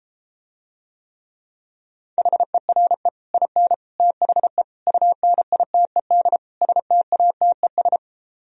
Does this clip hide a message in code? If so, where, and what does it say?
Yes: Morse code from 2.2 to 8.0 s, heard as a keyed tone, about 33 wpm, 700 Hz. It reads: HERE IN THE UNITED STATES